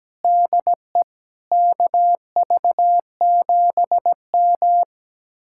Morse code, heard as a keyed tone, about 17 wpm, 700 Hz. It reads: DE KV7M
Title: Speed-Racing Common QSO Phrases - 17wpm